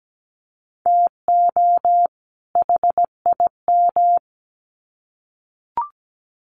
Morse code, heard as a keyed tone, about 17 wpm, 700 Hz. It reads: TO HIM
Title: Sets of 2 Words 17wpm